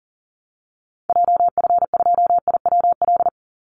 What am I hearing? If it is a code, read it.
Morse code, heard as a keyed tone, about 39 wpm, 700 Hz. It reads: JF2IWL
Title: Call Signs Repeated 3x - 40wpm